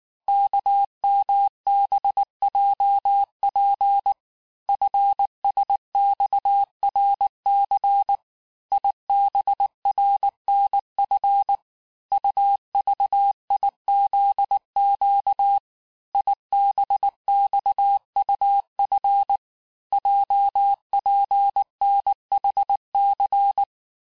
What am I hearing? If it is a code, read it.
Morse code, heard as a keyed tone, about 19 wpm, 800 Hz. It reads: KMBJP FSXRC IBRNF UVIZQ IBXUF JPNHC
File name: cours-du-lundi files / cours-1589-texte-chiffre-vitesse-18mpm